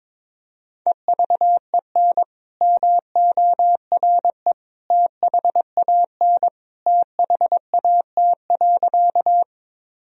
Morse code, heard as a keyed tone, about 22 wpm, 700 Hz. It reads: EVEN MORE THAN THAT.
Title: Sentences from Top 1000 Words - Review Part 1 - 22wpm